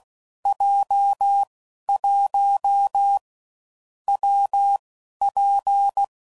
Morse code, heard as a keyed tone, about 16 wpm, 800 Hz. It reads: J 1 W P
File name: Combinations / Difficult Combos / WPJ WPJ1